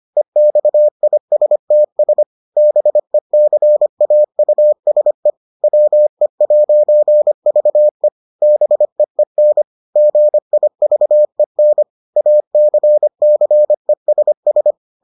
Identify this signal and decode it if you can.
Morse code, heard as a keyed tone, about 25 wpm, 600 Hz. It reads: EXISTS BECAUSE WE'VE BEEN GIVEN ACCESS